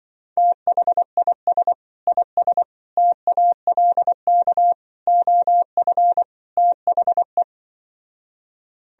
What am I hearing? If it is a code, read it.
Morse code, heard as a keyed tone, about 25 wpm, 700 Hz. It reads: THIS IS TALK OF THE